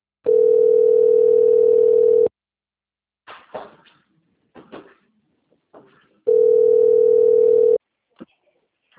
Ringback tone (North America), 440 + 480 Hz.